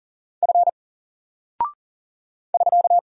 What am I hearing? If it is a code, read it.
Morse code, heard as a keyed tone, about 40 wpm, 700 Hz. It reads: F <SK>
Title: Single Character (Letter:Number:Punctuation:Prosign) 40wpm